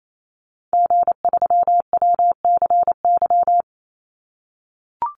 Morse code, heard as a keyed tone, about 28 wpm, 700 Hz. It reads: G3WCY
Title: Call Signs - 28wpm